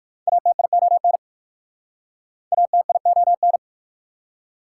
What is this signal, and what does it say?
Morse code, heard as a keyed tone, about 54 wpm, 700 Hz. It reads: ATION ATION
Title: Speed-Racing Prefixes and Suffixes 45wpm